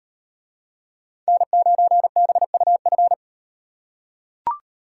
Morse code, heard as a keyed tone, about 38 wpm, 700 Hz. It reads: N9BUF